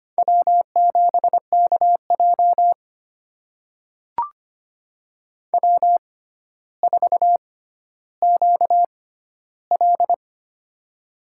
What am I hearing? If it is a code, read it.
Morse code, heard as a keyed tone, about 25 wpm, spaced as for 13 wpm, 700 Hz. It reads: W7KJ W4QL
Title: USA Call Signs - 6x Character Spacing - 25wpm